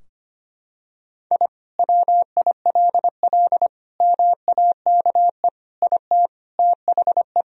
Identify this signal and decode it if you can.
Morse code, heard as a keyed tone, about 25 wpm, 700 Hz. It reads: I WILL MAKE IT THE
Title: Sentences from Top 1000 Words - Review Part 3 - 25wpm